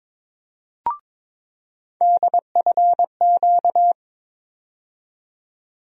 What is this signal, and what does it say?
Morse code, heard as a keyed tone, about 22 wpm, 700 Hz. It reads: DFQ
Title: USA Call Sign Postfixes - 3 Letters 22wpm